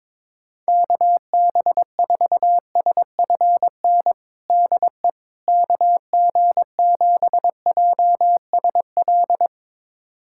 Morse code, heard as a keyed tone, about 22 wpm, 700 Hz. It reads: KB4SFN DE KG7JSL
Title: USA Callsign DE Callsign - 6 Characters - 22wpm